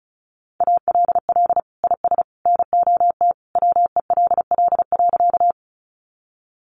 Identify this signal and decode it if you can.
Morse code, heard as a keyed tone, about 34 wpm, 700 Hz. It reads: ALL IS NOT WELL.
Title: Sentences from Top 600 Words 35wpm